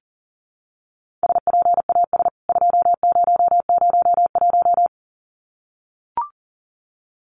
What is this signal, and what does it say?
Morse code, heard as a keyed tone, about 40 wpm, 700 Hz. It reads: SPAS 2001
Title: CWT Exchange 40wpm